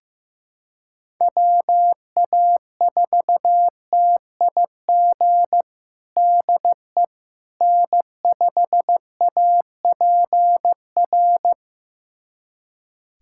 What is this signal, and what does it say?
Morse code, heard as a keyed tone, about 15 wpm, 700 Hz. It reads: WA4TIG DE N5APR
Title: USA Callsign DE Callsign 15wpm